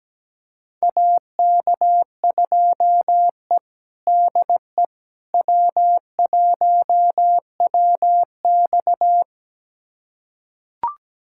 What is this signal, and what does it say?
Morse code, heard as a keyed tone, about 17 wpm, 700 Hz. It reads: AK2E DE W1WX
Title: USA Callsign DE Callsign - 4 Characters - 17wpm